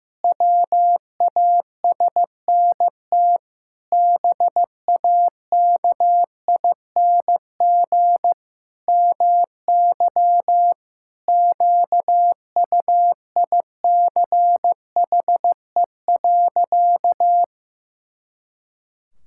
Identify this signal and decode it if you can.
Morse code, heard as a keyed tone, about 15 wpm, 700 Hz. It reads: WASNT BAKING MY QUICHE.
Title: Pangram Challenge v2 - 15wpm